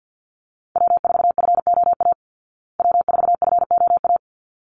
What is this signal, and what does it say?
Morse code, heard as a keyed tone, about 49 wpm, 700 Hz. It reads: W4FOA W4FOA
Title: Call Signs Repeated 3x - 50wpm